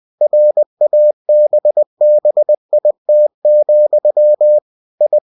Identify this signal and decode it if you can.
Morse code, heard as a keyed tone, about 20 wpm, 600 Hz. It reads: RABBIT, I
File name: cw-books / winnie-the-pooh / winnie-20wpm-page-0016